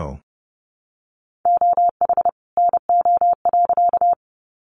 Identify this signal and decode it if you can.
Morse code, heard as a keyed tone, about 30 wpm, 700 Hz. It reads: OH NO.
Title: Additional Easy Phrases 30wpm